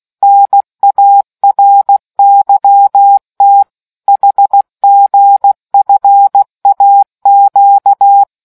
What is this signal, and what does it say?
Morse code, heard as a keyed tone, about 16 wpm, 800 Hz. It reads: NARYT HGFAQ